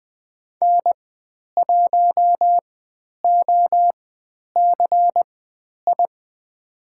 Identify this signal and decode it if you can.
Morse code, heard as a keyed tone, about 20 wpm, spaced as for 10 wpm, 700 Hz. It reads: N1OCI
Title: 095 Lesson 16g - Letter c - Call Signs 20wpm